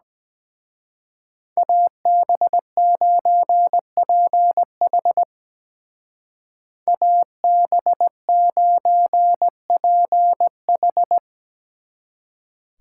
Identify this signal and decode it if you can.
Morse code, heard as a keyed tone, about 20 wpm, 700 Hz. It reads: AB9PH AB9PH
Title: Speed Racing - USA 2x2 Call Signs 17wpm